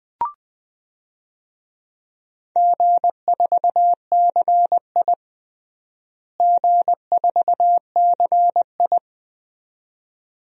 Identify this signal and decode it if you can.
Morse code, heard as a keyed tone, about 20 wpm, 700 Hz. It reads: G4CI G4CI
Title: DXCC-100 Call Signs - 4 Characters - Repeated 2x - 20wpm